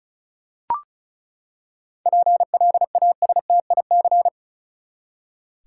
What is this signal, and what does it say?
Morse code, heard as a keyed tone, about 34 wpm, 700 Hz. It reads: PLASTIC
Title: Seven Letter Words - 35wpm